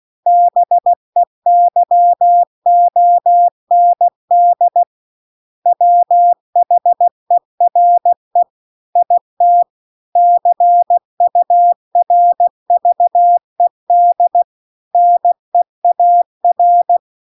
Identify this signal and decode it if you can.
Morse code, heard as a keyed tone, about 16 wpm, 700 Hz. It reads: BEYOND WHERE IT CURVED NEAR